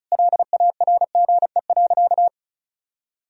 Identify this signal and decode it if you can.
Morse code, heard as a keyed tone, about 34 wpm, 700 Hz. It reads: LARGE.
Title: Sentences from Top 1000 Words - Review Part 1 - 35wpm